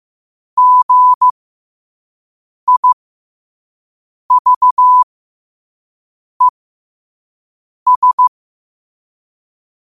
Morse code, heard as a keyed tone, about 15 wpm, spaced as for 5 wpm, 1000 Hz. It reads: GIVES
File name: qotdfiles / 5wpm